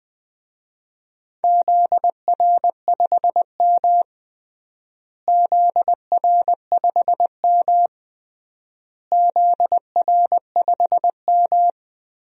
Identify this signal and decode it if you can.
Morse code, heard as a keyed tone, about 20 wpm, 700 Hz. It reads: ZR5M ZR5M ZR5M
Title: DXCC-100 Call Signs - 4 Characters - Repeated 3x - 20wpm